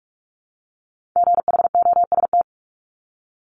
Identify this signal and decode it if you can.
Morse code, heard as a keyed tone, about 46 wpm, 700 Hz. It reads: GHOST
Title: Multi-letter Phonemes 45wpm